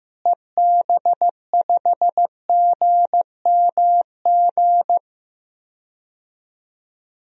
Morse code, heard as a keyed tone, about 15 wpm, 700 Hz. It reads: EB5GMG
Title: DXCC-100 Call Signs - 6 Characters - Repeated 3x - 15wpm